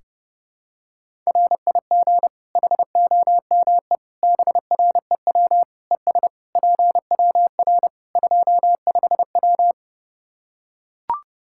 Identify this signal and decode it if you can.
Morse code, heard as a keyed tone, about 30 wpm, 700 Hz. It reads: RIG HOME BREW ES PWR 25W
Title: Practice QSO - Exchange 2 - Info - 30wpm